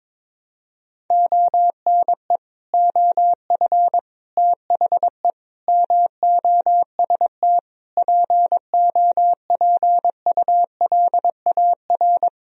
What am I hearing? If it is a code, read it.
Morse code, heard as a keyed tone, about 22 wpm, 700 Hz. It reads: ONE OF THE MOST POPULAR
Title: Speed-Racing - Sets of 5 Words 22wpm